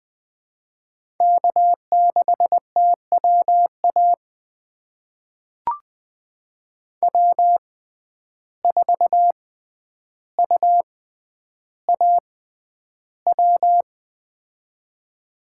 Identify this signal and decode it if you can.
Morse code, heard as a keyed tone, about 20 wpm, spaced as for 10 wpm, 700 Hz. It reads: K6TWA W4UAW
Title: USA Call Signs - 5 Characters - 6x Character Spacing - 20wpm